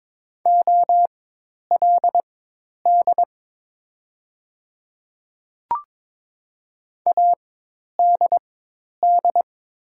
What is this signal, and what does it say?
Morse code, heard as a keyed tone, about 22 wpm, spaced as for 10 wpm, 690 Hz. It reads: OLD ADD
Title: 4x Character Spacing - Top 200 Words - 22wpm